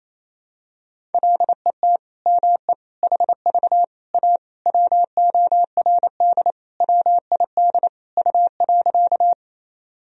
Morse code, heard as a keyed tone, about 28 wpm, 700 Hz. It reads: LET ME HV A WORD WID U.
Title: Sentences from Top 300 Words and Abbreviations 28wpm